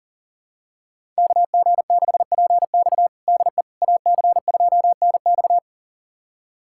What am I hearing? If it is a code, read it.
Morse code, heard as a keyed tone, about 40 wpm, 700 Hz. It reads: KG6PX DE AC2NX
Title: USA Callsign DE Callsign - 5 Characters - 40wpm